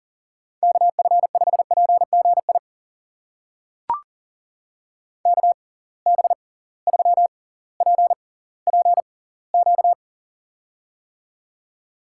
Morse code, heard as a keyed tone, about 38 wpm, spaced as for 20 wpm, 700 Hz. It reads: KF5PGI KB3PPQ
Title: USA Call Signs - 6 Characters - 6x Character Spacing - 40wpm